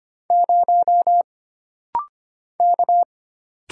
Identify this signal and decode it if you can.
Morse code, heard as a keyed tone, about 25 wpm, 700 Hz. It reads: 0 K
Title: Single Letter-Number Alt - Rapid-Fire 25wpm